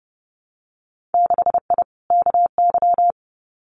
Morse code, heard as a keyed tone, about 30 wpm, 700 Hz. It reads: 6I KY
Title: Winter Field Day Exchanges 30wpm